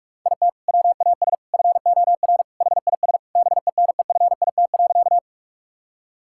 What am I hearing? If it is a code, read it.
Morse code, heard as a keyed tone, about 42 wpm, 700 Hz. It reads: IT WAS FOR HIS BENEFIT.
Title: Sentences from Top 1000 Words - Review Part 6 - 45wpm